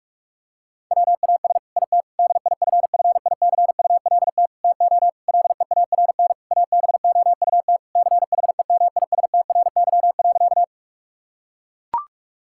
Morse code, heard as a keyed tone, about 41 wpm, 700 Hz. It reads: WAS IT DIFFICULT TO LEARN ABOUT CHEMISTRY.
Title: Sentences from Top 1000 Words - Review Part 6 - 45wpm